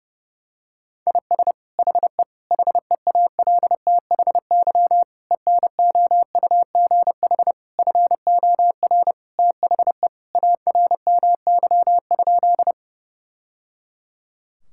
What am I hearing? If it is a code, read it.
Morse code, heard as a keyed tone, about 32 wpm, 700 Hz. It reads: IS HE HEALTHY ENOUGH FOR THE ARMY?
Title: Sentences from Top 1000 Words - Review Part 5 - 30wpm